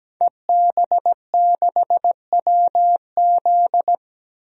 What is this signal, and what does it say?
Morse code, heard as a keyed tone, about 17 wpm, 700 Hz. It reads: EB6WZ
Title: DXCC-100 Call Signs - 5 Characters - Repeated 3x - 17wpm